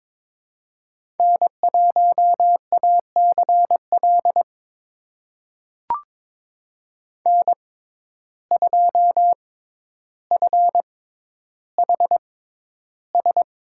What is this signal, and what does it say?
Morse code, heard as a keyed tone, about 22 wpm, spaced as for 11 wpm, 700 Hz. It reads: N1ACL N2FHS
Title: USA Call Signs - 5 Characters - 6x Character Spacing - 22wpm